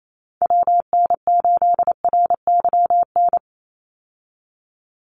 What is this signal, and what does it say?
Morse code, heard as a keyed tone, about 28 wpm, 700 Hz. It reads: WN8RYN
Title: USA Call Signs 28wpm